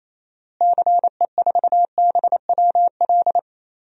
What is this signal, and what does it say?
Morse code, heard as a keyed tone, about 28 wpm, 700 Hz. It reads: CE4BWL
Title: DXCC-100 Call Signs - 6 Characters - Repeated 2x - 28wpm